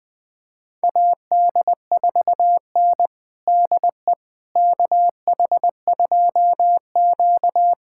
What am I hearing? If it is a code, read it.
Morse code, heard as a keyed tone, about 20 wpm, 700 Hz. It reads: AD4N DE KH2Q
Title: USA Callsign DE Callsign - 4 Characters - 20wpm